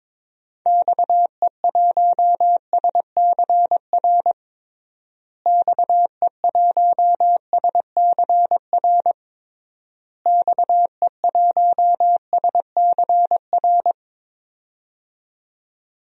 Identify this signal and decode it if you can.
Morse code, heard as a keyed tone, about 22 wpm, 700 Hz. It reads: XE1SCR XE1SCR XE1SCR
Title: DXCC-100 Call Signs - 6 Characters - Repeated 3x - 22wpm